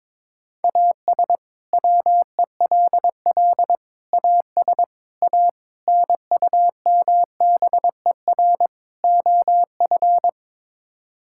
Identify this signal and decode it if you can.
Morse code, heard as a keyed tone, about 22 wpm, 700 Hz. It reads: AS WELL AS A NUMBER OF
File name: practice / 22wpm / Sets of 6 Words - 22wpm